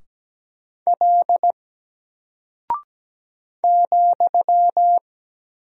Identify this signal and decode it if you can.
Morse code, heard as a keyed tone, about 17 wpm, 700 Hz. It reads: L ,
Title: Single Character (Letter:Number:Punctuation:Prosign) 17wpm